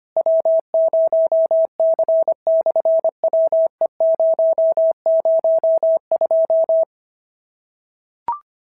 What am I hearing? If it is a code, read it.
Morse code, heard as a keyed tone, about 25 wpm, 650 Hz. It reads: W0C/WE002
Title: SOTA QSO Practice 25wpm